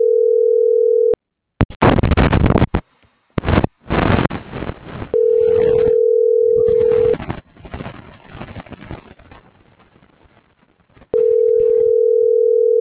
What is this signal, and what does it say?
Ringback tone (North America), 440 + 480 Hz.